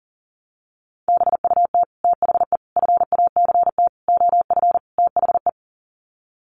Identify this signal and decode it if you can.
Morse code, heard as a keyed tone, about 40 wpm, 700 Hz. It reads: BUT THE FACT OF THE
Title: Sets of 5 Words Encore2 40wpm